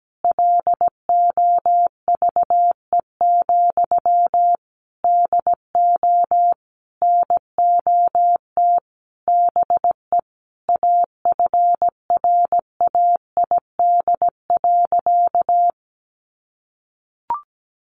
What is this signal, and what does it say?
Morse code, heard as a keyed tone, about 17 wpm, 700 Hz. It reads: LOVE, DO NOT BE AFRAID.
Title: Sentences from Top 1500 Words 17wpm